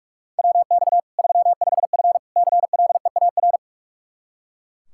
Morse code, heard as a keyed tone, about 41 wpm, 700 Hz. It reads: WX 35F CLEAR
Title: K1USN SST QSO Practice 45wpm